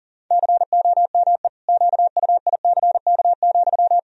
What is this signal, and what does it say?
Morse code, heard as a keyed tone, about 37 wpm, 700 Hz. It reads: COME QUICK,
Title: Sentences from Top 1000 Words - Review Part 3 - 40wpm